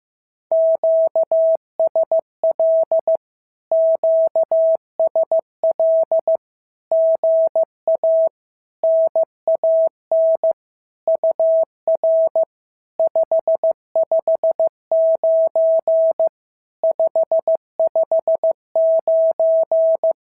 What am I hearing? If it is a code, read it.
Morse code, heard as a keyed tone, about 15 wpm, 650 Hz. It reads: QSL QSL GA NAN UR 559 559
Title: SOTA QSO Practice 15wpm